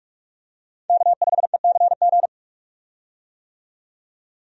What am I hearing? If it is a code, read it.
Morse code, heard as a keyed tone, about 43 wpm, 700 Hz. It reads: K5ECG
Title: USA Call Signs - 5 Characters - 45wpm